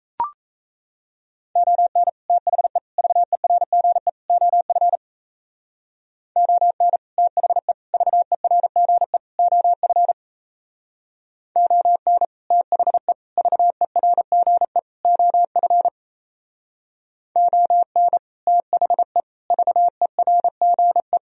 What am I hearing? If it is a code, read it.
Morse code, heard as a keyed tone, about 36 wpm, 700 Hz. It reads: ON THE VERGE OF ON THE VERGE OF ON THE VERGE OF ON THE VERGE